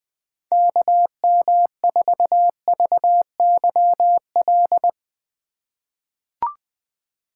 Morse code, heard as a keyed tone, about 21 wpm, 700 Hz. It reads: KM4VYL
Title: USA Call Signs - 4x Character Spacing - 20wpm